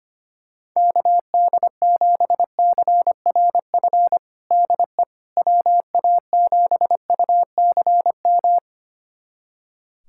Morse code, heard as a keyed tone, about 25 wpm, 700 Hz. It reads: KD7CRF DE WA7UCM
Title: USA Callsign DE Callsign - 6 Characters - 25wpm